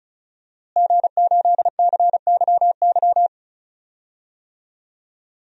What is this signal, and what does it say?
Morse code, heard as a keyed tone, about 34 wpm, 700 Hz. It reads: G8CYY
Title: DXCC-100 Call Signs - 5 Characters - Repeated 3x - 35wpm